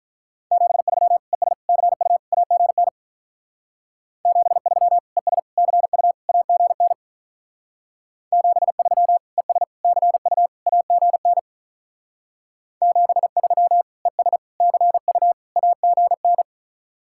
Morse code, heard as a keyed tone, about 46 wpm, 700 Hz. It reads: 73 ES CU AGN 73 ES CU AGN 73 ES CU AGN 73 ES CU AGN